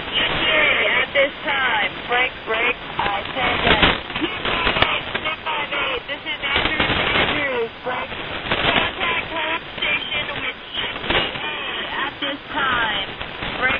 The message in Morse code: none

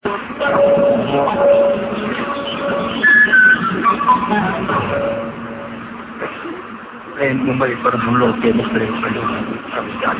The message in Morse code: none

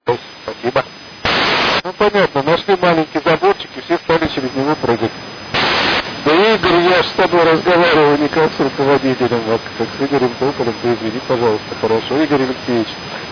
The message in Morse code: none

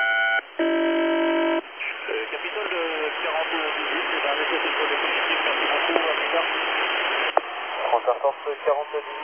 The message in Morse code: none

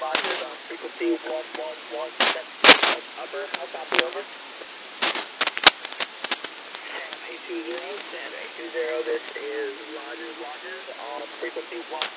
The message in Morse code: none

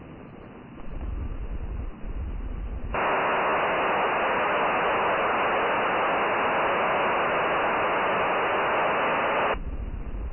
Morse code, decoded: none

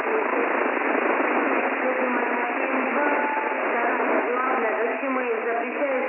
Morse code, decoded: none

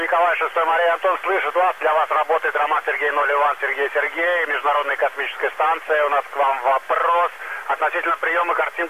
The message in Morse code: none

